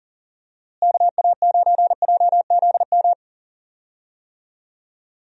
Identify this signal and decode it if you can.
Morse code, heard as a keyed tone, about 40 wpm, 700 Hz. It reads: KA9JZM
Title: USA Call Signs - 6 Characters - Repeated 2x - 40wpm